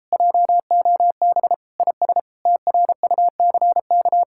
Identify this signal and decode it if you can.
Morse code, heard as a keyed tone, about 33 wpm, 700 Hz. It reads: JOB IS TRUCK